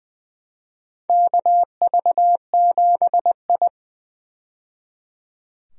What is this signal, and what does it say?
Morse code, heard as a keyed tone, about 20 wpm, 700 Hz. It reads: KV7I
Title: USA Call Signs - 4 Characters - 20wpm